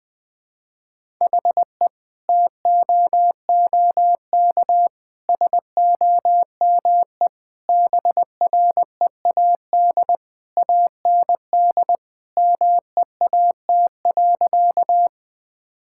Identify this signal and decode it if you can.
Morse code, heard as a keyed tone, about 21 wpm, 700 Hz. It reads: HE TOOK SOME BREAD AND MEAT.
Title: Sentences from Top 1600 Words 20wpm